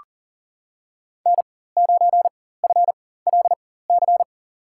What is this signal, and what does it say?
Morse code, heard as a keyed tone, about 40 wpm, spaced as for 19 wpm, 700 Hz. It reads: N9FLC
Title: USA Call Signs - 5 Characters - 4x Character Spacing - 40wpm